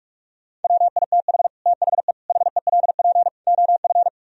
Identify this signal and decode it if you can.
Morse code, heard as a keyed tone, about 49 wpm, 700 Hz. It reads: WITH THE HELP OF